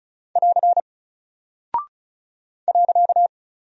Morse code, heard as a keyed tone, about 34 wpm, 700 Hz. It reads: <AR> .
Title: Single Character (Letter:Number:Punctuation:Prosign) Alt 35wpm